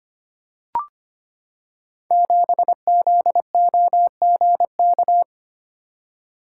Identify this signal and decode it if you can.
Morse code, heard as a keyed tone, about 25 wpm, 700 Hz. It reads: 7ZOGK
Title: Code Groups - 5 Letter:Numbers - 25wpm